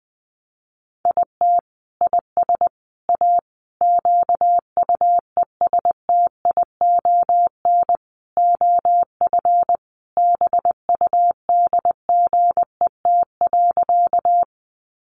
Morse code, heard as a keyed tone, about 20 wpm, 700 Hz. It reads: IT IS A QUESTION OF BUDGET.
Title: Sentences from Top 1200 Words 20wpm